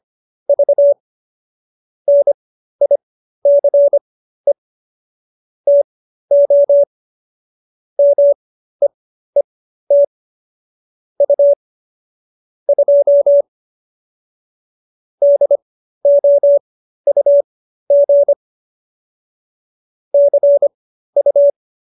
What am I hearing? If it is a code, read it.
Morse code, heard as a keyed tone, about 25 wpm, spaced as for 13 wpm, 570 Hz. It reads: V NICE TO MEET U 2 DOUG CU